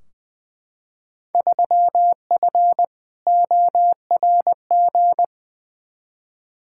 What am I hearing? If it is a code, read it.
Morse code, heard as a keyed tone, about 21 wpm, 700 Hz. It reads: 3F ORG
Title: Field Day Exchanges - 20wpm